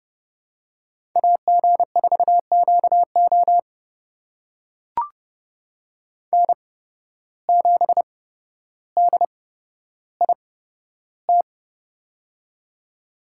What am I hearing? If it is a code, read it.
Morse code, heard as a keyed tone, about 30 wpm, spaced as for 13 wpm, 700 Hz. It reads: AG4QO N7DIT
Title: USA Call Signs - 5 Characters - 8x Character Spacing - 30wpm